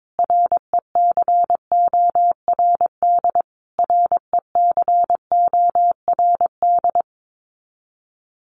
Morse code, heard as a keyed tone, about 22 wpm, 700 Hz. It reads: RECORD RECORD
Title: Top 500 Words Repeated 22wpm